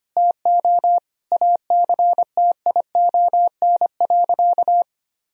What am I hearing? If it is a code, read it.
Morse code, heard as a keyed tone, about 25 wpm, 700 Hz. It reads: TO ACTION.
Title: Sentences from Top 1000 Words - Review Part 4 - 25wpm